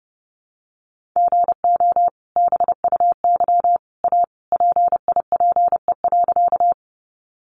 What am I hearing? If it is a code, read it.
Morse code, heard as a keyed tone, about 30 wpm, 700 Hz. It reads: GO BUY A PIPE.
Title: Sentences from Top 1500 Words 30wpm